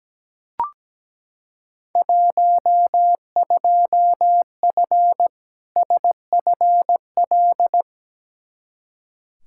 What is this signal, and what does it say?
Morse code, heard as a keyed tone, about 17 wpm, 700 Hz. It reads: 12F SFL
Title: Field Day Exchanges Encore - 17wpm